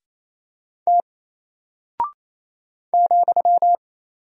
Morse code, heard as a keyed tone, about 27 wpm, 700 Hz. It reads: T ,
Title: Single Character (Letter:Number:Punctuation:Prosign) 28wpm